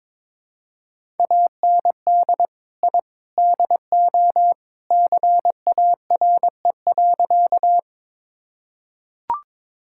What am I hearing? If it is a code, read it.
Morse code, heard as a keyed tone, about 22 wpm, 700 Hz. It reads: AND I DO CARE.
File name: practice / 22wpm / Sentences with 3-Word Phrases - 22wpm